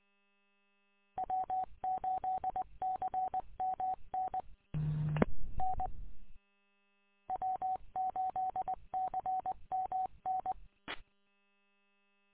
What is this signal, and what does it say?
Morse code, heard as a keyed tone, about 20 wpm, 750 Hz. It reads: W8CMN N W8CMN
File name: w8cmn- -combined